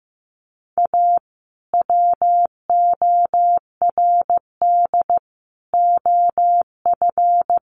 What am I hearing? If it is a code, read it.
Morse code, heard as a keyed tone, about 15 wpm, 700 Hz. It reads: A WORD OF
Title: Sentences from Top 1100 Words 15wpm